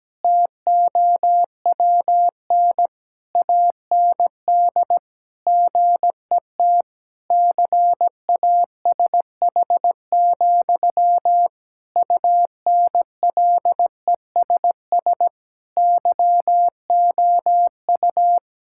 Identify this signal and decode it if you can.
Morse code, heard as a keyed tone, about 17 wpm, 700 Hz. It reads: TOWN AND GET CASH, UNLESS YOU
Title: Sentences from Top 1000 Words - Review Part 6 - 17wpm